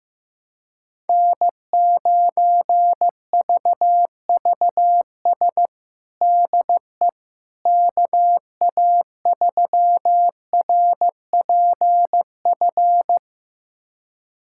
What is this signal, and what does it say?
Morse code, heard as a keyed tone, about 15 wpm, 700 Hz. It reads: N9VVS DE KA3RPF
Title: USA Callsign DE Callsign 15wpm